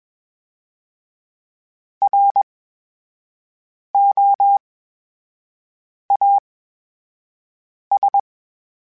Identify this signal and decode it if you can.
Morse code, heard as a keyed tone, about 21 wpm, spaced as for 5 wpm, 800 Hz. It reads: ROAS
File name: Special INGs-mp3s / Special INGs-mp3s / VerySlowToFaster Ing @20